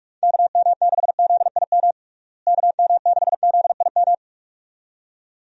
Morse code, heard as a keyed tone, about 42 wpm, 700 Hz. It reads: KM6ZIM KM6ZIM
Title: USA Call Signs - 6 Characters - Repeated 2x - 45wpm